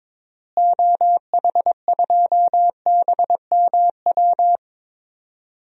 Morse code, heard as a keyed tone, about 22 wpm, 700 Hz. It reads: OH2BMW
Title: DXCC-100 Call Signs - 6 Characters - Repeated 2x - 22wpm